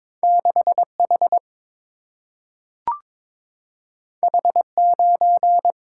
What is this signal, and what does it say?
Morse code, heard as a keyed tone, about 22 wpm, 700 Hz. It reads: 6H H9
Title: Single-Letter Single-Number 22wpm